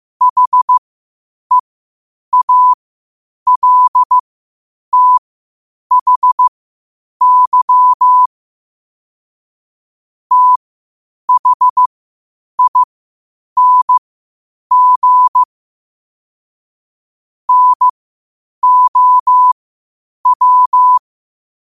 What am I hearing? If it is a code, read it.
Morse code, heard as a keyed tone, about 15 wpm, spaced as for 8 wpm, 1000 Hz. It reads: HEALTHY THING NOW
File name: qotdfiles / 8wpm